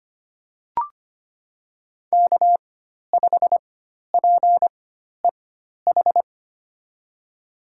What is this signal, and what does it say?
Morse code, heard as a keyed tone, about 25 wpm, spaced as for 12 wpm, 700 Hz. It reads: K5PEH
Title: USA Call Signs - 5 Characters - 4x Character Spacing - 25wpm